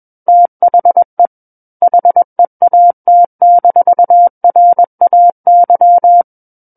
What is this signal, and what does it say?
Morse code, heard as a keyed tone, about 21 wpm, 700 Hz. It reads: THE HEAT-RAY
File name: Book One / 12-Chapter 12 - 21 WPM